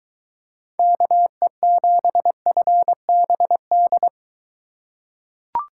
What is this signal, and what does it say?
Morse code, heard as a keyed tone, about 21 wpm, 700 Hz. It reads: KE7FBD